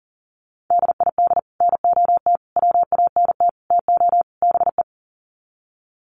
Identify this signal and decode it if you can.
Morse code, heard as a keyed tone, about 38 wpm, 700 Hz. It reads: DID NOT WANT TO BE
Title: Sets of 5 Words 40wpm